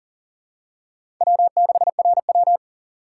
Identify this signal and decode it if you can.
Morse code, heard as a keyed tone, about 40 wpm, 700 Hz. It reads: W6RW
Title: USA Call Signs - 4 Characters - 4x Character Spacing - 40wpm